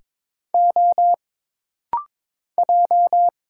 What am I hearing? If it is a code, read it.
Morse code, heard as a keyed tone, about 22 wpm, 700 Hz. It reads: O J
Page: This rapid-fire timing is geared to push your ICR (Instant Character Recognition) skills.